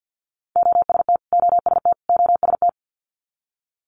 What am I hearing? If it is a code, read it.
Morse code, heard as a keyed tone, about 50 wpm, 700 Hz. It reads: OST OST OST
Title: Multi-letter Phonemes Repeated 3x 50wpm